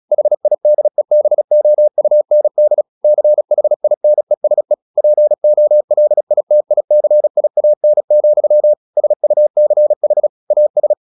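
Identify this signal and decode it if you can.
Morse code, heard as a keyed tone, about 36 wpm, 600 Hz. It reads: HIDEBOUND CHINESE POLITICIAN, SUCH AS
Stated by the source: Morse practice at 36 wpm, text of Ken Follett's novel Never, from CW med Gnister.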